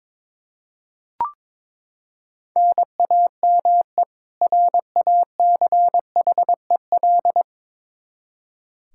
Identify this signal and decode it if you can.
Morse code, heard as a keyed tone, about 21 wpm, 700 Hz. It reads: NAME RACHEL
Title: Practice QSO - Exchange 1 - Beginning - 22wpm